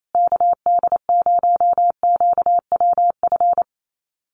Morse code, heard as a keyed tone, about 28 wpm, 700 Hz. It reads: KD0QWF
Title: USA Call Signs Repeated 3x 28wpm